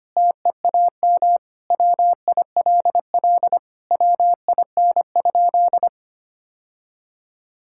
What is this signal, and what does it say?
Morse code, heard as a keyed tone, about 25 wpm, 700 Hz. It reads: TEAM WILL WIN?
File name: practice / 25wpm / Sentences with 4-Word Phrases - 25wpm